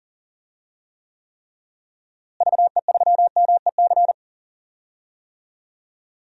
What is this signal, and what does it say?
Morse code, heard as a keyed tone, about 40 wpm, 700 Hz. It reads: VE3MEC
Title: DXCC-100 Call Signs - 6 Characters - Repeated 3x - 40wpm